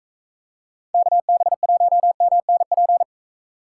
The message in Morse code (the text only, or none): KB1MNP